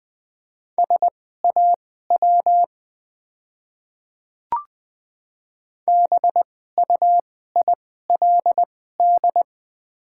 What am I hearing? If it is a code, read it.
Morse code, heard as a keyed tone, about 20 wpm, spaced as for 14 wpm, 700 Hz. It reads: SAW BUILD
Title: 2x Character Spacing - Top 300 Words - 20wpm